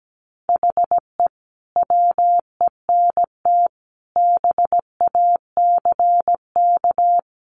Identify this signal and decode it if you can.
Morse code, heard as a keyed tone, about 17 wpm, 700 Hz. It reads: HE WENT BACK
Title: Sentences from Top 1100 Words 17wpm